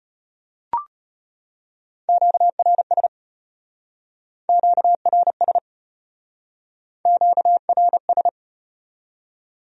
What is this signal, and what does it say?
Morse code, heard as a keyed tone, about 34 wpm, 700 Hz. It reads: QRS QRS QRS